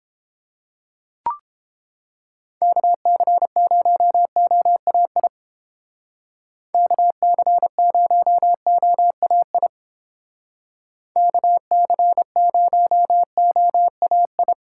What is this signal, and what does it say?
Morse code, heard as a keyed tone, about 31 wpm, 700 Hz. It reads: KC0OAI KC0OAI KC0OAI